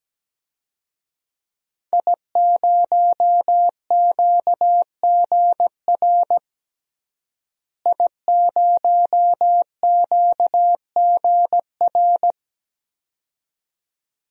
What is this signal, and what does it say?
Morse code, heard as a keyed tone, about 17 wpm, 700 Hz. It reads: I0QGR I0QGR
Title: DXCC-100 Call Signs - 5 Characters - Repeated 2x - 17wpm